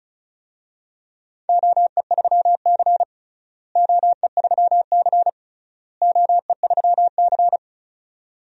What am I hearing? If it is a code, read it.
Morse code, heard as a keyed tone, about 36 wpm, 700 Hz. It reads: OE3C OE3C OE3C
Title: DXCC-100 Call Signs - 4 Characters - Repeated 3x - 35wpm